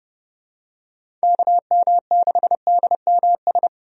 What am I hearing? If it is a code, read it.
Morse code, heard as a keyed tone, about 30 wpm, 700 Hz. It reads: KM6DMS
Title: USA Call Signs - 6 Characters - Repeated 3x - 30wpm